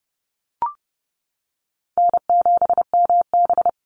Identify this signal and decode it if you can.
Morse code, heard as a keyed tone, about 30 wpm, 700 Hz. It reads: N7MB
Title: Call Signs - 30wpm